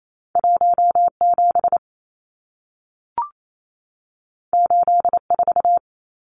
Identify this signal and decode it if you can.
Morse code, heard as a keyed tone, about 28 wpm, 700 Hz. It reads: 17 84
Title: Numbers - 2-Digit 28wpm